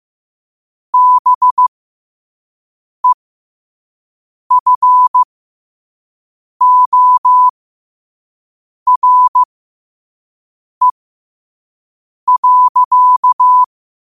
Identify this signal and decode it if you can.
Morse code, heard as a keyed tone, about 15 wpm, spaced as for 5 wpm, 1000 Hz. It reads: BEFORE.
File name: qotdfiles / 5wpm